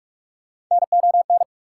Morse code, heard as a keyed tone, about 48 wpm, 690 Hz. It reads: NON
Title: Prefixes and Suffixes 45wpm